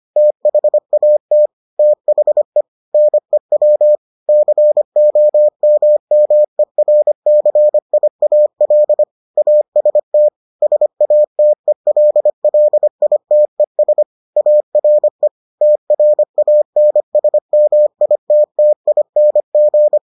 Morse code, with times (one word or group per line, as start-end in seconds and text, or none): 0.2-1.5 THAT
1.8-2.6 THE
2.9-4.0 NEW
4.3-9.0 COMMERCIAL
9.4-10.3 AST
10.6-14.0 SATELLITES
14.4-15.3 ARE
15.6-20.0 TRANSMITTING